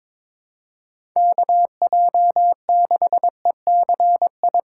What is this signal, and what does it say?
Morse code, heard as a keyed tone, about 21 wpm, 700 Hz. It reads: KJ6ECI
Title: USA Call Signs - 6 Characters - 8x Character Spacing - 22wpm